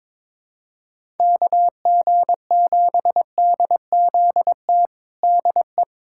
Morse code, heard as a keyed tone, about 21 wpm, 700 Hz. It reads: KG7DZT DE
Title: USA Callsign DE Callsign - 6 Characters - 22wpm